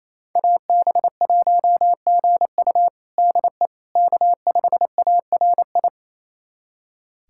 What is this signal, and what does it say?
Morse code, heard as a keyed tone, about 28 wpm, 700 Hz. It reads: AB1GU DE K5ARI
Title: USA Callsign DE Callsign - 5 Characters - 28wpm